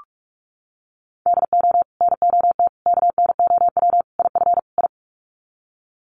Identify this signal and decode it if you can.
Morse code, heard as a keyed tone, about 44 wpm, 700 Hz. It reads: DO NOT KNOW IF I
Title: Sets of 5 Words 45wpm